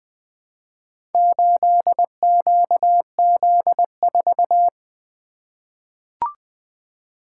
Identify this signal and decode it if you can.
Morse code, heard as a keyed tone, about 20 wpm, 700 Hz. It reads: 8QZ4
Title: Four Letter-Number Combinations - 20wpm